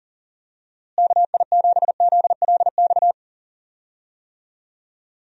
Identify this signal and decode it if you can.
Morse code, heard as a keyed tone, about 40 wpm, 700 Hz. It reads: KI7ZLX
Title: USA Call Signs - 6 Characters - Repeated 2x - 40wpm